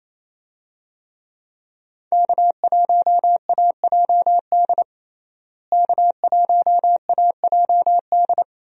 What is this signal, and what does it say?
Morse code, heard as a keyed tone, about 27 wpm, 700 Hz. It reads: K1AJD K1AJD
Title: USA Call Signs - 5 Characters - Repeated 2x - 28wpm